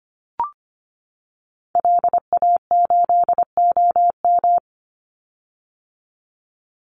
Morse code, heard as a keyed tone, about 25 wpm, 700 Hz. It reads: LA8OM
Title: Call Signs - 25wpm